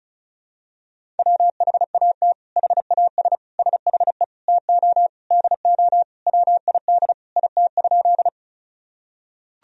Morse code, heard as a keyed tone, about 36 wpm, 700 Hz. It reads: WHAT HAS SHE TO DO WID IT?
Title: Sentences from Top 100 Words and Abbreviations 35wpm